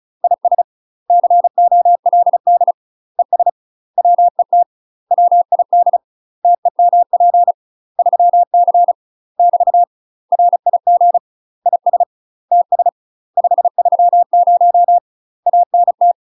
Morse code, heard as a keyed tone, about 36 wpm, 700 Hz. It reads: IS COLD ES WET WID TEMP 3C <BT> RIG IS TS 530 ANT
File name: UK MASTER QSOs TRAINING / 35WPM / UK MASTER QSOs 0038